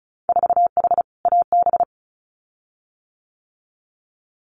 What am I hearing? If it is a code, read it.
Morse code, heard as a keyed tone, about 34 wpm, 700 Hz. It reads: 4H AB
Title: Winter Field Day Exchanges Encore 35wpm